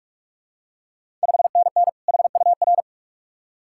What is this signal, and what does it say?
Morse code, heard as a keyed tone, about 46 wpm, 690 Hz. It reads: 5NN HUR
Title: Ontario QSO Party Exchange 45wpm